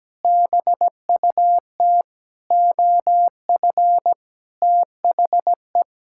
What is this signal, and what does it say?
Morse code, heard as a keyed tone, about 17 wpm, 700 Hz. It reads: BUT OF THE
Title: Sentences from Top 1600 Words 17wpm